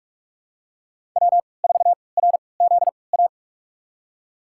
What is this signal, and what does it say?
Morse code, heard as a keyed tone, about 46 wpm, spaced as for 26 wpm, 700 Hz. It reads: W4RZA
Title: USA Call Signs - 3x Character Spacing - 45wpm